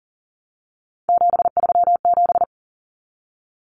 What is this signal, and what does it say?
Morse code, heard as a keyed tone, about 40 wpm, 700 Hz. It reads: 737
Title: Common Cut Numbers - 3-Digit 40wpm